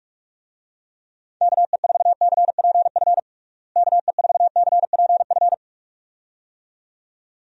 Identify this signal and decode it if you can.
Morse code, heard as a keyed tone, about 41 wpm, 700 Hz. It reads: KE4CPF KE4CPF
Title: USA Call Signs - 6 Characters - Repeated 2x - 45wpm